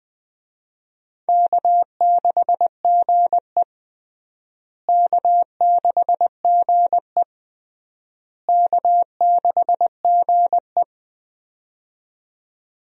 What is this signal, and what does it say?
Morse code, heard as a keyed tone, about 20 wpm, 700 Hz. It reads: K6GE K6GE K6GE
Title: USA Call Signs - 4 Characters - Repeated 3x - 20wpm